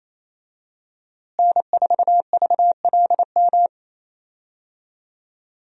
Morse code, heard as a keyed tone, about 28 wpm, 700 Hz. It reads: N4VLM
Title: USA Call Signs - 5 Characters - Repeated 2x - 28wpm